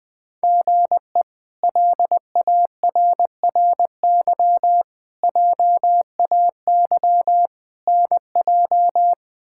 Morse code, heard as a keyed tone, about 20 wpm, 700 Hz. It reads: GE LARRY JAY NJ
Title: K1USN SST QSO Practice - 20wpm